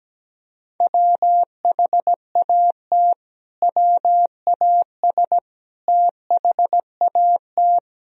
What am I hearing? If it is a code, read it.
Morse code, heard as a keyed tone, about 17 wpm, 700 Hz. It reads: WHAT WAS THAT
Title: Sentences from Top 1000 Words - Review Part 1 - 17wpm